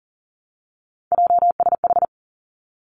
Morse code, heard as a keyed tone, about 42 wpm, 700 Hz. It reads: JSH
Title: Three Challenging Letter-Number Combinations 40wpm